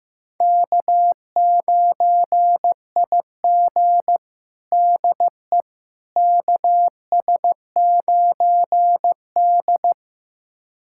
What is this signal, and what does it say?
Morse code, heard as a keyed tone, about 15 wpm, 700 Hz. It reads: K9IG DE KS9D
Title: USA Callsign DE Callsign - 4 Characters - 15wpm